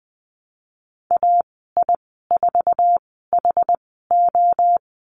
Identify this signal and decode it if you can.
Morse code, heard as a keyed tone, about 20 wpm, spaced as for 14 wpm, 700 Hz. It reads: AI4HO
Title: 069 Lesson 12g - Letter d - Call Signs 20wpm